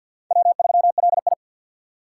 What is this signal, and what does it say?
Morse code, heard as a keyed tone, about 50 wpm, 700 Hz. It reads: W3LI
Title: USA Call Signs - 4 Characters - 2x Character Spacing - 50wpm